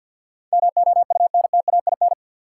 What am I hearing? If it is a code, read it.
Morse code, heard as a keyed tone, about 52 wpm, 700 Hz. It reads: MOUNTAIN
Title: Sentences from Top 1300 Words - New Word Drill Alt - 50wpm